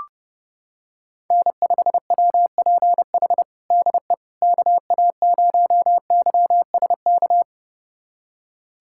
Morse code, heard as a keyed tone, about 30 wpm, 700 Hz. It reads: N5WPH DE KA0YSK
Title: USA Callsign DE Callsign 30wpm